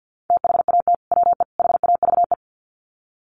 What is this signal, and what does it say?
Morse code, heard as a keyed tone, about 50 wpm, 700 Hz. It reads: THAT WE HAVE
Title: Sets of 3 Words 50wpm